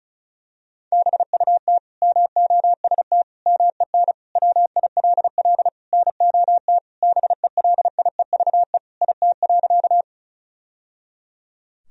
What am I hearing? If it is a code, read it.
Morse code, heard as a keyed tone, about 34 wpm, 700 Hz. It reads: BUT MOST MEN WILL NOT BELIEVE IT.
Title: Sentences from Top 1000 Words - Review Part 1 - 35wpm